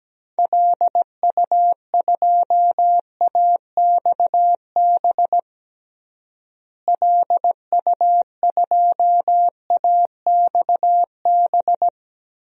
Morse code, heard as a keyed tone, about 17 wpm, 700 Hz. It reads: LU2AXB LU2AXB
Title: DXCC-100 Call Signs - 6 Characters - Repeated 3x - 17wpm